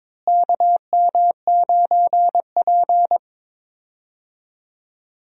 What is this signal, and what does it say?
Morse code, heard as a keyed tone, about 21 wpm, 700 Hz. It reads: KM9P
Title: USA Call Signs - 4 Characters - Repeated 2x - 22wpm